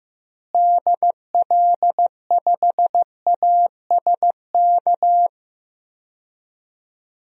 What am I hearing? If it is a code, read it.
Morse code, heard as a keyed tone, about 15 wpm, 700 Hz. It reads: DL5ASK
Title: DXCC-100 Call Signs - 6 Characters - Repeated 3x - 15wpm